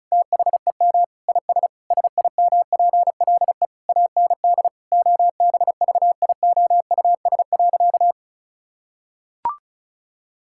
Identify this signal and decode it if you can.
Morse code, heard as a keyed tone, about 36 wpm, 700 Hz. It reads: THEM IS SIMPLE AND OBVIOUS.
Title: Sentences from Top 1600 Words 35wpm